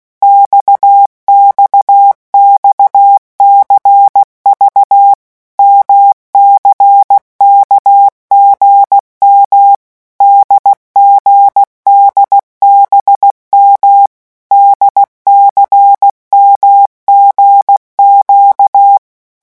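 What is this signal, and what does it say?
Morse code, heard as a keyed tone, about 16 wpm, 800 Hz. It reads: XXXCV MCKGM DGDBM DCMGQ
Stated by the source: Morse letter-group revision drill at 15 words per minute.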